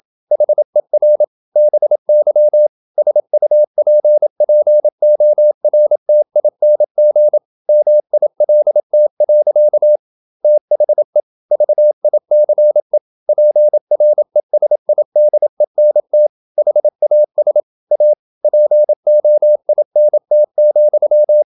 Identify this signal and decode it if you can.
Morse code, heard as a keyed tone, about 28 wpm, 600 Hz. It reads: HER BY SUPPORTING MILT. THE VICE PRESIDENT HAS A POINT,
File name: Never 0027wpm